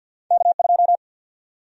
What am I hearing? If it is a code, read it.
Morse code, heard as a keyed tone, about 52 wpm, 700 Hz. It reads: K2
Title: USA Call Sign Prefixes - 50wpm